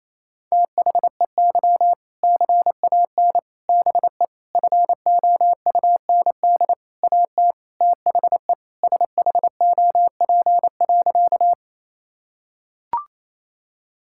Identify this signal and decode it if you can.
Morse code, heard as a keyed tone, about 28 wpm, 700 Hz. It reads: THEY CAN BE FOUND AT THE SHOP.
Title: Sentences from Top 1600 Words 28wpm